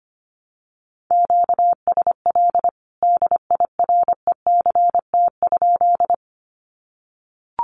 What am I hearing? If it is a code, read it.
Morse code, heard as a keyed tone, about 25 wpm, 700 Hz. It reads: QSL DIRECT?
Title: Common QSO Phrases 25wpm